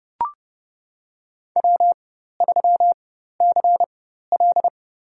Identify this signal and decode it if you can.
Morse code, heard as a keyed tone, about 32 wpm, spaced as for 14 wpm, 700 Hz. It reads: W3CL
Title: USA Call Signs - 4 Characters - 4x Character Spacing - 30wpm